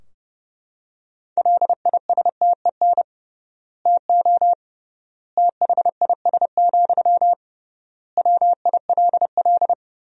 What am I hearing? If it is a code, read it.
Morse code, heard as a keyed tone, about 30 wpm, 700 Hz. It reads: LISTEN TO THIS, WILL
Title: 3x Word-Spacing - Sentences from Top 400 Words - 30wpm